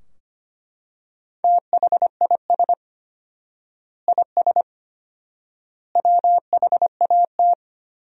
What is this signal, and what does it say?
Morse code, heard as a keyed tone, about 25 wpm, 700 Hz. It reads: THIS IS WHAT
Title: 4x Word-Spacing - Sentences from Top 300 Words - 25wpm